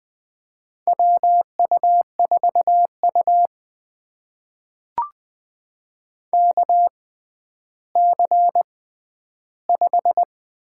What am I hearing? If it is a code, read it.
Morse code, heard as a keyed tone, about 20 wpm, spaced as for 11 wpm, 700 Hz. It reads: WU4U KC5
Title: USA Call Signs - 6x Character Spacing - 20wpm